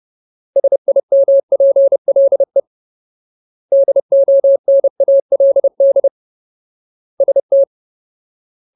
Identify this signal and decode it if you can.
Morse code, heard as a keyed tone, about 30 wpm, 550 Hz. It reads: SIMPLE DONALD ST
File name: Ultimate Word List / 30WPM Ultimate